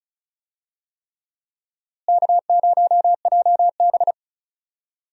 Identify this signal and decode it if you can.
Morse code, heard as a keyed tone, about 36 wpm, 700 Hz. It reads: K0JB
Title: USA Call Signs - 4 Characters - Repeated 3x - 35wpm